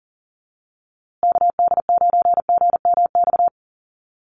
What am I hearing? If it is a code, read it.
Morse code, heard as a keyed tone, about 40 wpm, 700 Hz. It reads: KD9GMX
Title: USA Call Signs Repeated 3x 40wpm